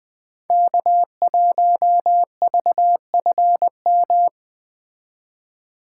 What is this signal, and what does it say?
Morse code, heard as a keyed tone, about 20 wpm, 700 Hz. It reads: K1VFM
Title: Speed Racing - USA 1x3 Call Signs 17wpm